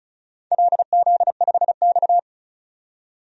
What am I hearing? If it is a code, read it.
Morse code, heard as a keyed tone, about 36 wpm, 690 Hz. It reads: LZ5X
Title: DXCC-100 Call Signs - 4 Characters - 35wpm